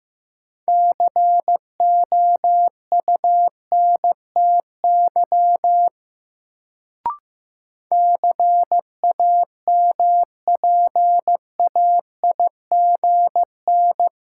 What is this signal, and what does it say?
Morse code, heard as a keyed tone, about 15 wpm, 700 Hz. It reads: COUNTY CAMPAIGN
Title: Sentences from Top 1200 Words - New Word Drill - 15wpm